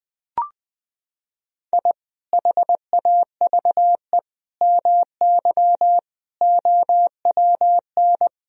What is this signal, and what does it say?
Morse code, heard as a keyed tone, about 21 wpm, 700 Hz. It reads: I HAVE MY OWN
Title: Sentences from Top 1000 Words - Review Part 3 - 20wpm